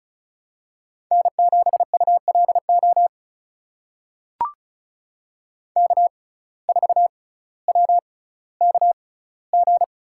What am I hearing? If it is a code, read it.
Morse code, heard as a keyed tone, about 36 wpm, spaced as for 18 wpm, 700 Hz. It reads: N7ULO K4WKG
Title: USA Call Signs - 5 Characters - 6x Character Spacing - 35wpm